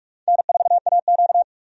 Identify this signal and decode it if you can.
Morse code, heard as a keyed tone, about 46 wpm, 700 Hz. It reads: N4AQ
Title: USA Call Signs - 4 Characters - 45wpm